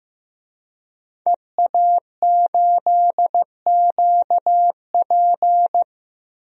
Morse code, heard as a keyed tone, about 16 wpm, 700 Hz. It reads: EA8QP
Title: DXCC-100 Call Signs - 5 Characters - Repeated 3x - 15wpm